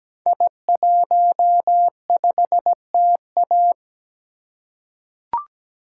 Morse code, heard as a keyed tone, about 17 wpm, 700 Hz. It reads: I15TA
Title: Code Groups - 5 Letter:Numbers - 17wpm